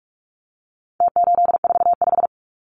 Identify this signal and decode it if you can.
Morse code, heard as a keyed tone, about 44 wpm, 700 Hz. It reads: T845
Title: Numbers - Contest Style - 4-Digit 45wpm